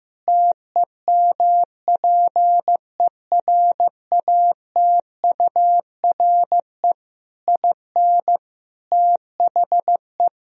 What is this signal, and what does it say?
Morse code, heard as a keyed tone, about 16 wpm, 700 Hz. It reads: TEMPERATURE IN THE
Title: Sentences from Top 1000 Words - Review Part 4 - 15wpm